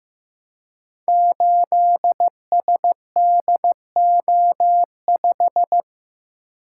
Morse code, heard as a keyed tone, about 15 wpm, 700 Hz. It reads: 8SDO5
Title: Code Groups - 5 Letter:Numbers - 15wpm